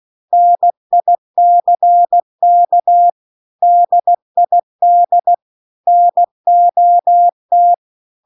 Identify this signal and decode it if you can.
Morse code, heard as a keyed tone, about 16 wpm, 700 Hz. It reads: NICK DID NOT
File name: CW-The-Big-Twoheaded-River-2-16WPM